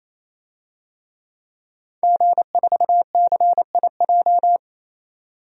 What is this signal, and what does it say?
Morse code, heard as a keyed tone, about 28 wpm, 700 Hz. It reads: G4CIJ
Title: DXCC-100 Call Signs - 5 Characters - 28wpm